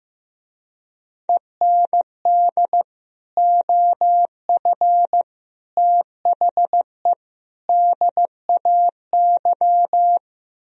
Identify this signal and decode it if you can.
Morse code, heard as a keyed tone, about 16 wpm, 700 Hz. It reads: END OF THE DAY
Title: Speed-Racing - Sets of 4 Words 15wpm